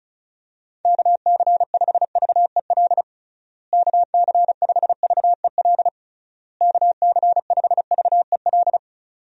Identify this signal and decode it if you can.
Morse code, heard as a keyed tone, about 36 wpm, 700 Hz. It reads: KC5VEL KC5VEL KC5VEL
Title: USA Call Signs - 6 Characters - Repeated 3x - 35wpm